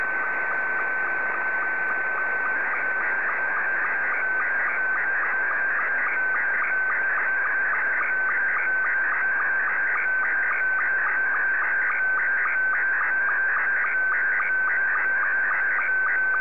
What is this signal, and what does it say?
SSTV (slow-scan television), Scottie 2.
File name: sstv / S2-VK3MCG